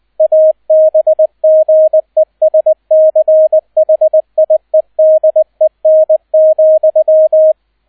Morse code, heard as a keyed tone, about 20 wpm, 630 Hz, with radio noise. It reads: ABGESCHIEDEN,
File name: 002 Zweiter Streich 20wpm